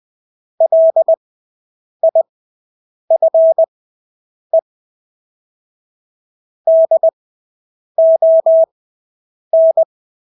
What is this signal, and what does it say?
Morse code, heard as a keyed tone, about 20 wpm, spaced as for 8 wpm, 650 Hz. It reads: LIFE DON
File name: Engelske ord 20/8 wpm